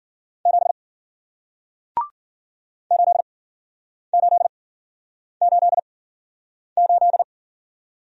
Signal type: Morse code

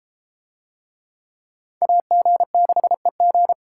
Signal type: Morse code

